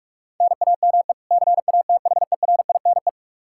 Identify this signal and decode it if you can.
Morse code, heard as a keyed tone, about 44 wpm, 700 Hz. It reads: NAME CATHERINE
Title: Practice QSO - Exchange 1 - Beginning - 45wpm